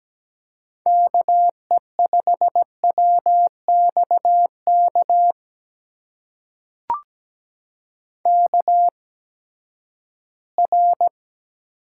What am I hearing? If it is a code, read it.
Morse code, heard as a keyed tone, about 17 wpm, spaced as for 12 wpm, 700 Hz. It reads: KE5WXK KR
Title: USA Call Signs - 6 Characters - 8x Character Spacing - 17wpm